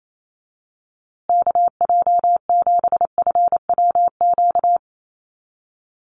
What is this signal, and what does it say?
Morse code, heard as a keyed tone, about 28 wpm, 700 Hz. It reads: KJ7FWQ
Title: USA Call Signs Repeated 2x 28wpm